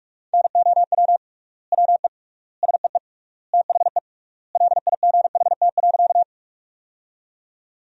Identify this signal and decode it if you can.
Morse code, heard as a keyed tone, about 46 wpm, 700 Hz. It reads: NOW WE SEE THE LIGHT.
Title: 3x Word-Spacing - Sentences from Top 200 Words - 45wpm